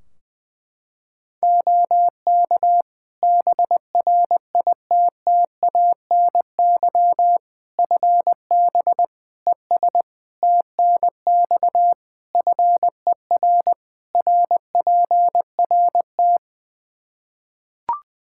Morse code, heard as a keyed tone, about 20 wpm, 700 Hz. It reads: OK BRITTANY FB ES TNX FER RPRT
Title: K1USN SST QSO Practice 20wpm